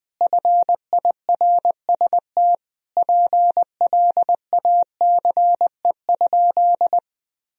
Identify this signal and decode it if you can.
Morse code, heard as a keyed tone, about 21 wpm, 700 Hz. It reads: FIRST PLACE?
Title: Sentences from Top 200 Words and Abbreviations 20wpm